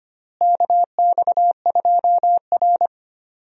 Morse code, heard as a keyed tone, about 25 wpm, 700 Hz. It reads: KX2R
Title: USA Call Signs - 4 Characters - Repeated 3x - 25wpm